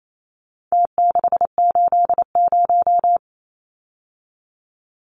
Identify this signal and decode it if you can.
Morse code, heard as a keyed tone, about 28 wpm, 700 Hz. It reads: T680
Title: Numbers - Contest Style - 4-Digit 28wpm